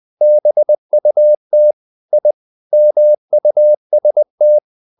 Morse code, heard as a keyed tone, about 20 wpm, 600 Hz. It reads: BUT I MUST
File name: the old man and the sea part 2 20wpm